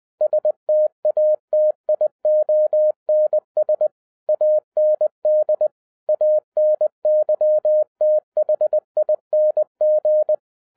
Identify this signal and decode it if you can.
Morse code, heard as a keyed tone, about 20 wpm, 610 Hz. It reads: STATIONS AND ANYTHING
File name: 2011-07-mp3 / n08 A LOT HAS CHANGED 20wpm